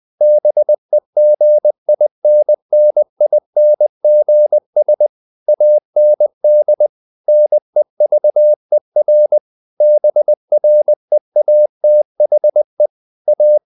Morse code, heard as a keyed tone, about 20 wpm, 600 Hz. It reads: BEGINNINGS AND NEVER BREATHE A
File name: CW / Practice / If20